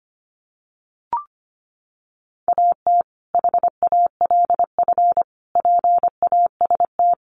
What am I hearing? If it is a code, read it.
Morse code, heard as a keyed tone, about 25 wpm, 700 Hz. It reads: AT HALF PAST
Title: Sentences from Top 600 Words 25wpm